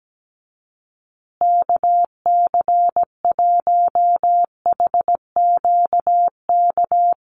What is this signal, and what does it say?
Morse code, heard as a keyed tone, about 17 wpm, 700 Hz. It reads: KC1HQK
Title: USA Call Signs Repeated 3x 17wpm